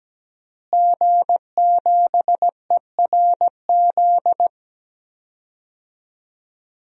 Morse code, heard as a keyed tone, about 17 wpm, 700 Hz. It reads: G7ERZ
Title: DXCC-100 Call Signs - 5 Characters - Repeated 3x - 17wpm